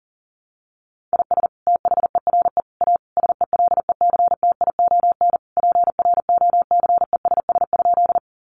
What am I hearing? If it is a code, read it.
Morse code, heard as a keyed tone, about 40 wpm, 700 Hz. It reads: IS THERE A SELECTION PROCESS?
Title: Sentences from Top 1400 Words 40wpm